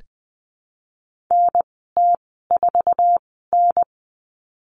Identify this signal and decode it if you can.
Morse code, heard as a keyed tone, about 20 wpm, spaced as for 15 wpm, 700 Hz. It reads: NT4N
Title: 088 Lesson 15g - Number 5 - Call Signs 20wpm